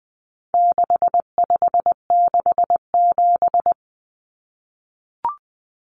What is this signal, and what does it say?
Morse code, heard as a keyed tone, about 20 wpm, 700 Hz. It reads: 6567
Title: Numbers - 4-Digit 20wpm